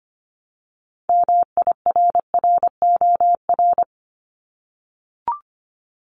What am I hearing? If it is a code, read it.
Morse code, heard as a keyed tone, about 25 wpm, 700 Hz. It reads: MIRROR
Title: Multi-letter Phonemes 25wpm